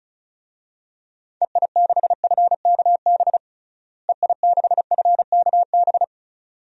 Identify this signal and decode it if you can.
Morse code, heard as a keyed tone, about 36 wpm, 700 Hz. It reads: EI6FKB EI6FKB
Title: DXCC-100 Call Signs - 6 Characters - Repeated 2x - 35wpm